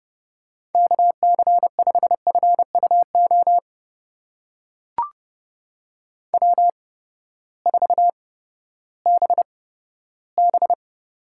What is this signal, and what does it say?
Morse code, heard as a keyed tone, about 31 wpm, spaced as for 15 wpm, 700 Hz. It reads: KC5FUO W4BB
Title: USA Call Signs - 8x Character Spacing - 30wpm